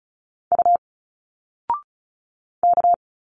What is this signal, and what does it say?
Morse code, heard as a keyed tone, about 36 wpm, 700 Hz. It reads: U K
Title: Single Letters 35wpm